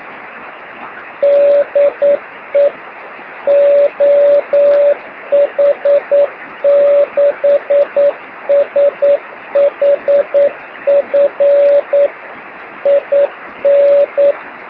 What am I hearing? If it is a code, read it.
Morse code, heard as a keyed tone, about 9 wpm, 570 Hz, with radio noise. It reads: DE OH6SHF IN
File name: sm7lcb / oh6shf 23cm